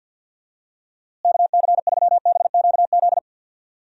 Morse code, heard as a keyed tone, about 52 wpm, 700 Hz. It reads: KC3BQZ
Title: USA Call Signs - 6 Characters - Repeated 2x - 50wpm